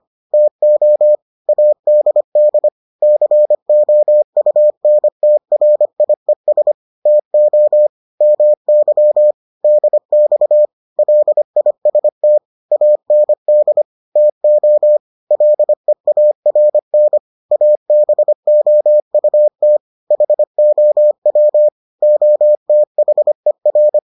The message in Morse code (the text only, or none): TO ADD COUNTRIES TO MY DX LIST AND TO LEARN ABOUT HOW OTHER